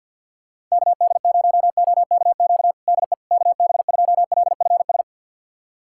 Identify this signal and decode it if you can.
Morse code, heard as a keyed tone, about 49 wpm, 700 Hz. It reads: KD0OKQ DE KB2LFS
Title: USA Callsign DE Callsign - 6 Characters - 50wpm